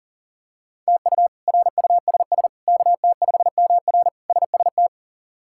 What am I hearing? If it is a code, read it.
Morse code, heard as a keyed tone, about 37 wpm, 690 Hz. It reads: TU RUSS KT5MR SST
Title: K1USN SST QSO Practice - 40wpm